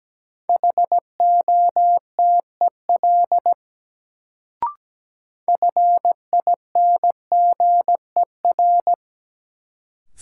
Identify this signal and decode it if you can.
Morse code, heard as a keyed tone, about 17 wpm, 700 Hz. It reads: HOTEL FINGER
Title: Sentences from Top 1600 Words - New Word Drill - 17wpm